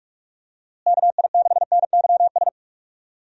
Morse code, heard as a keyed tone, about 48 wpm, 700 Hz. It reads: KI6NYS
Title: USA Call Signs - 6 Characters - Repeated 3x - 45wpm